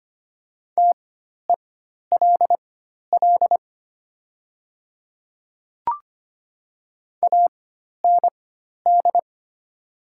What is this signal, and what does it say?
Morse code, heard as a keyed tone, about 25 wpm, spaced as for 12 wpm, 700 Hz. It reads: TELL AND
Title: 4x Character Spacing - Top 200 Words - 25wpm